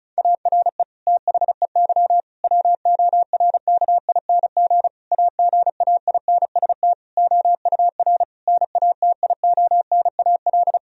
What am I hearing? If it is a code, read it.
Morse code, heard as a keyed tone, about 36 wpm, 700 Hz. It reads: ARE THEY WORKING AGAINST OUR NATIONAL
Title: Sentences from Top 1000 Words - Review Part 4 - 35wpm